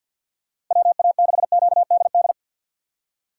Morse code, heard as a keyed tone, about 50 wpm, 690 Hz. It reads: WA6QDD
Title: USA Call Signs - 6 Characters - 3x Character Spacing - 50wpm